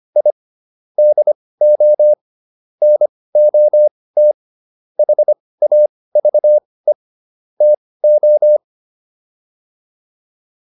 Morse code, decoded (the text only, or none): I DO NOT HAVE TO